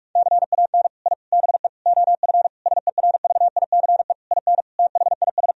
Morse code, heard as a keyed tone, about 46 wpm, 700 Hz. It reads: CAN I BE OF SERVICE IN THIS
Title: Sentences from Top 1000 Words - Review Part 4 - 45wpm